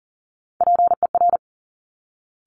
Morse code, heard as a keyed tone, about 38 wpm, 700 Hz. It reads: PER
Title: Top Three Letter Combinations in Words 40wpm